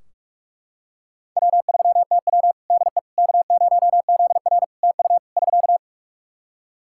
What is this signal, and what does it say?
Morse code, heard as a keyed tone, about 42 wpm, 700 Hz. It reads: W3TW DE K0ZR TU <SK>
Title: K1USN SST QSO Practice 45wpm